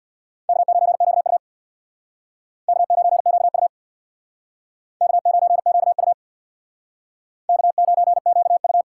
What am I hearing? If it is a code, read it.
Morse code, heard as a keyed tone, about 63 wpm, 700 Hz. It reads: K9QU K9QU K9QU K9QU